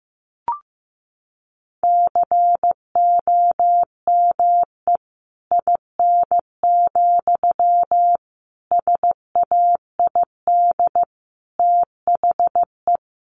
Morse code, heard as a keyed tone, about 15 wpm, 700 Hz. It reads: COME IN, SAID THE
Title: Sentences from Top 1000 Words 15wpm